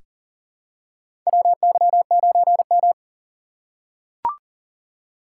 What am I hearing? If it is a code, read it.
Morse code, heard as a keyed tone, about 37 wpm, 700 Hz. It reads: WY9M
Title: USA Call Signs - 4 Characters - 3x Character Spacing - 40wpm